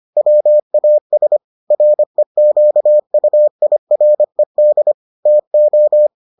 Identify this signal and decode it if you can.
Morse code, heard as a keyed tone, about 25 wpm, 600 Hz. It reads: WAS REQUIRED TO